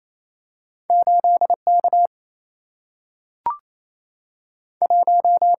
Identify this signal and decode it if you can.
Morse code, heard as a keyed tone, about 28 wpm, 700 Hz. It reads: 8 K 1
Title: Single-Letter Single-Number 28wpm